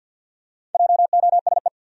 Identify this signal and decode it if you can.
Morse code, heard as a keyed tone, about 51 wpm, 700 Hz. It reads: JOSE
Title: USA Common Names - Max 5 Letters - 50wpm